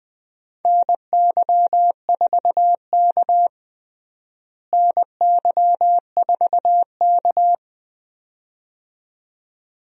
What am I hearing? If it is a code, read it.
Morse code, heard as a keyed tone, about 20 wpm, 700 Hz. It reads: NY4K NY4K
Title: USA Call Signs - 4 Characters - Repeated 2x - 20wpm